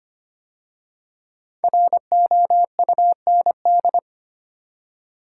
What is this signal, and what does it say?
Morse code, heard as a keyed tone, about 25 wpm, 700 Hz. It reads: ROUND
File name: 5 Letter Words - First 3 Letters Given - 25wpm